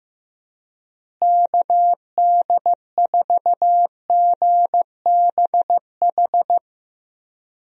Morse code, heard as a keyed tone, about 15 wpm, 700 Hz. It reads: KD4GBH
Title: USA Call Signs - 6 Characters - Repeated 2x - 15wpm